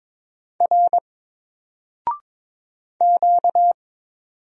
Morse code, heard as a keyed tone, about 22 wpm, 700 Hz. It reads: R Q
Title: Single Character (Letter:Number:Punctuation:Prosign) Alt 22wpm